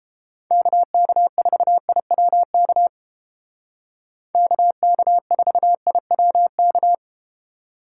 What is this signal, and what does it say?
Morse code, heard as a keyed tone, about 32 wpm, 700 Hz. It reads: KK4IWK KK4IWK